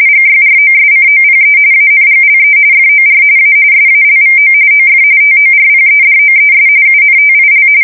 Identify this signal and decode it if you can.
RTTY, about 45.45 baud. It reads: GYROSCOPE ANGLE FOR A SALVO OF TORPEDOES UNTIL
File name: fsk170 / usnavy4